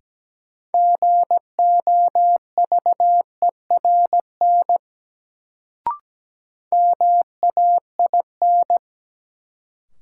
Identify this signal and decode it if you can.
Morse code, heard as a keyed tone, about 17 wpm, 700 Hz. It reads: GOVERN MAIN
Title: Sentences from Top 1000 Words - Review Part 2 - New Word Drill - 17wpm